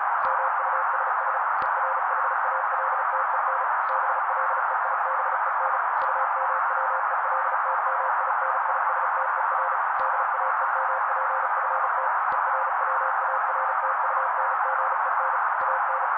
Morse code, heard as a keyed tone, about 36 wpm, 530 Hz. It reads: MUSS ERHALTEN BLEIBEN UMWELTMINISTER DAEMPFT ERWARTUNGEN AN
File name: daily / 35wpm noise